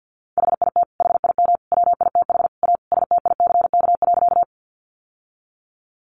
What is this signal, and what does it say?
Morse code, heard as a keyed tone, about 50 wpm, 700 Hz. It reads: HIT HIM WITH A STICK.
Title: Sentences from Top 1200 Words 50wpm